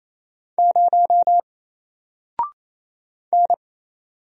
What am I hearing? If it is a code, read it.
Morse code, heard as a keyed tone, about 27 wpm, 700 Hz. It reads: 0 N
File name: Single-Letter-Number-28wpm